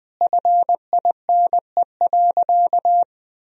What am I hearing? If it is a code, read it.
Morse code, heard as a keyed tone, about 20 wpm, 700 Hz. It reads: FINE.
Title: Sentences from Top 1000 Words - Review Part 2 - 20wpm